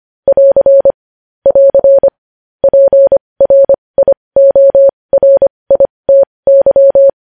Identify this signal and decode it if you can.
Morse code, heard as a keyed tone, about 25 wpm, spaced as for 20 wpm, 550 Hz. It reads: <AR> <AR> PRIORITY
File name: podcast / News headlines-1x-25x20